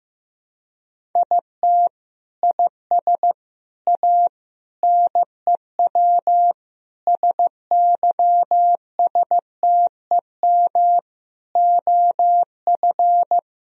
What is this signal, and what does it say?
Morse code, heard as a keyed tone, about 15 wpm, 700 Hz. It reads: IT IS A NEW SYSTEM OF
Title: Sentences from Top 1000 Words - Review Part 3 - 15wpm